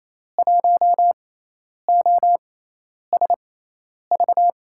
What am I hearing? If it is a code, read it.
Morse code, heard as a keyed tone, about 28 wpm, spaced as for 10 wpm, 700 Hz. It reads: 1OSV
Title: USA Call Signs - 5 Characters - 6x Character Spacing - 28wpm